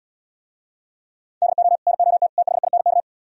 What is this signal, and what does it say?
Morse code, heard as a keyed tone, about 77 wpm, 700 Hz. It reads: DO NOT THINK